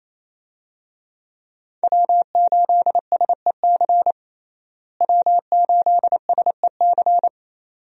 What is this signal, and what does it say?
Morse code, heard as a keyed tone, about 28 wpm, 700 Hz. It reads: W8SEC W8SEC
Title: USA Call Signs - 5 Characters - Repeated 2x - 28wpm